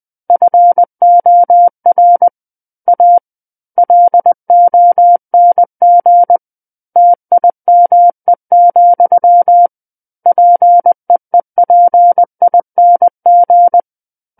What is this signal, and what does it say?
Morse code, heard as a keyed tone, about 20 wpm, 700 Hz. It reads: FOR A LONG TIME, PEEPING
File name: 20 WPM